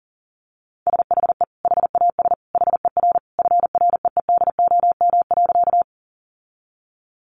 Morse code, heard as a keyed tone, about 40 wpm, 700 Hz. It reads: SHE HAS HER FREEDOM.
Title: Sentences from Top 800 Words 40wpm